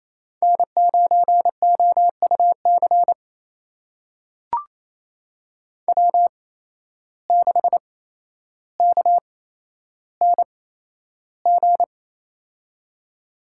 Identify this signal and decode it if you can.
Morse code, heard as a keyed tone, about 28 wpm, spaced as for 12 wpm, 700 Hz. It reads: N9OUC W6KNG
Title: USA Call Signs - 5 Characters - 8x Character Spacing - 28wpm